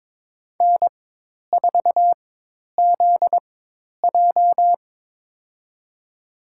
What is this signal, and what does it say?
Morse code, heard as a keyed tone, about 22 wpm, spaced as for 10 wpm, 700 Hz. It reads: N4ZJ
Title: USA Call Signs - 4 Characters - 4x Character Spacing - 22wpm